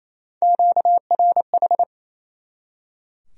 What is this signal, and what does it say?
Morse code, heard as a keyed tone, about 35 wpm, 700 Hz. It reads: QRH
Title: Speed-Racing Q-Codes - 28wpm